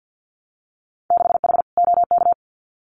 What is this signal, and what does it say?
Morse code, heard as a keyed tone, about 50 wpm, 700 Hz. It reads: 6H OK
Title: Winter Field Day Exchanges Encore 50wpm